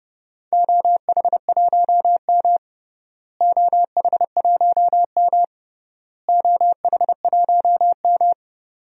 Morse code, heard as a keyed tone, about 32 wpm, 700 Hz. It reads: OH1M OH1M OH1M
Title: DXCC-100 Call Signs - 4 Characters - Repeated 3x - 30wpm